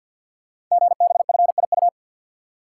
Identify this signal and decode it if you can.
Morse code, heard as a keyed tone, about 50 wpm, 700 Hz. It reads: GBFIU
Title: Code Groups - 5 Letters - 50wpm